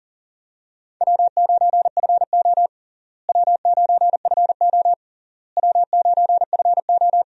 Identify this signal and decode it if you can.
Morse code, heard as a keyed tone, about 37 wpm, 700 Hz. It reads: W9FO W9FO W9FO
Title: USA Call Signs - 4 Characters - Repeated 3x - 40wpm